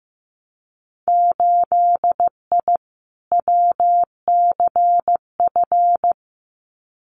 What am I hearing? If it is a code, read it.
Morse code, heard as a keyed tone, about 15 wpm, 700 Hz. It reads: 8I WCF
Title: Winter Field Day Exchanges Encore 15wpm